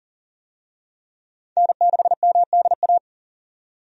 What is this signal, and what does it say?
Morse code, heard as a keyed tone, about 40 wpm, 700 Hz. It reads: N6MDA
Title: USA Call Signs - 5 Characters - 40wpm